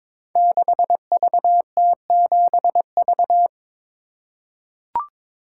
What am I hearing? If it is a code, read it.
Morse code, heard as a keyed tone, about 22 wpm, 700 Hz. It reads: 6VT7V
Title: Code Groups - 5 Letter:Numbers - 22wpm